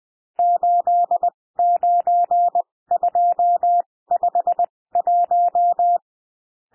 Morse code, heard as a keyed tone, about 20 wpm, spaced as for 16 wpm, 700 Hz. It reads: 89251
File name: cwmp3 / N1 SOUND NUMBERS 20 wpm